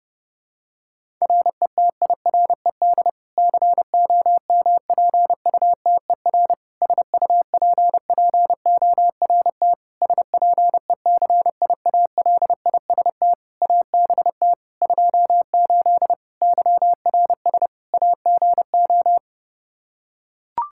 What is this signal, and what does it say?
Morse code, heard as a keyed tone, about 30 wpm, 700 Hz. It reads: RETIRED COMPUTER SUPPORT SPECIALIST ABT 28 YRS AGO
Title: Practice QSO - Exchange 3 - More Info - 30wpm